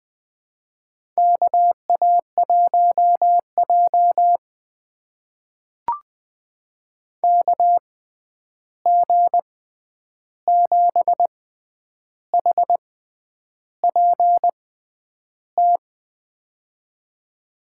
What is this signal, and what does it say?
Morse code, heard as a keyed tone, about 20 wpm, spaced as for 9 wpm, 700 Hz. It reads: KA1J KG7HPT
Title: USA Call Signs - 6x Character Spacing - 20wpm